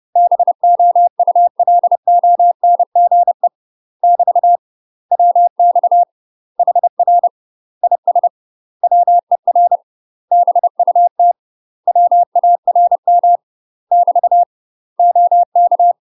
Morse code, heard as a keyed tone, about 30 wpm, 700 Hz. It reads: BOULONGE <BT> WX HR IS WER BUT WARM <BT> OK
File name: UK MASTER QSOs TRAINING / 30WPM / UK MASTER QSOs 0010